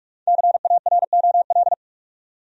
Morse code, heard as a keyed tone, about 48 wpm, 700 Hz. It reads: CAROL
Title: USA Common Names - Max 5 Letters - 45wpm